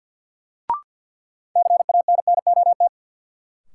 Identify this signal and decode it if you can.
Morse code, heard as a keyed tone, about 47 wpm, 700 Hz. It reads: CANNOT
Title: Sentences from Top 1000 Words - Review Part 6 - New Word Drill - 50wpm